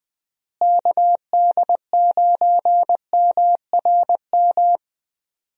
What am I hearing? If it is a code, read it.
Morse code, heard as a keyed tone, about 21 wpm, 700 Hz. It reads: KD9MRM
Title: USA Call Signs - 6 Characters - Repeated 2x - 20wpm